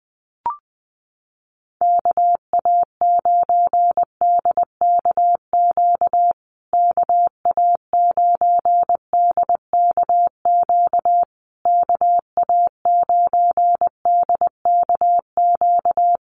Morse code, heard as a keyed tone, about 19 wpm, 700 Hz. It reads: KA9DKQ KA9DKQ KA9DKQ
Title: USA Call Signs Repeated 3x 20wpm